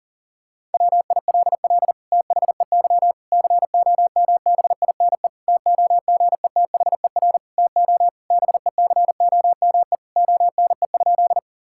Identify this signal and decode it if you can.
Morse code, heard as a keyed tone, about 37 wpm, 700 Hz. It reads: WILL THEY COMBINE TOGETHER TO BECOME ONE?
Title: Sentences from Top 1600 Words 40wpm